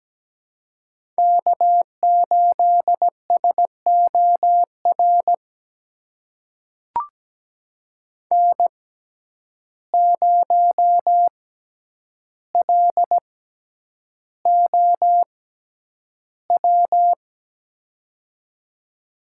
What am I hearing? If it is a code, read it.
Morse code, heard as a keyed tone, about 17 wpm, spaced as for 9 wpm, 700 Hz. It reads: K8SOR N0LOW
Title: USA Call Signs - 5 Characters - 6x Character Spacing - 17wpm